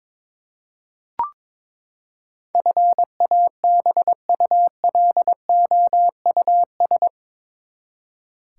Morse code, heard as a keyed tone, about 22 wpm, 700 Hz. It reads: FABULOUS
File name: practice / 22wpm / Eight Letter Words - 22wpm